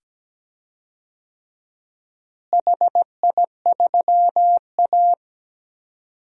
Morse code, heard as a keyed tone, about 17 wpm, 700 Hz. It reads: HI3A
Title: DXCC-100 Call Signs - 4 Characters - Repeated 2x - 17wpm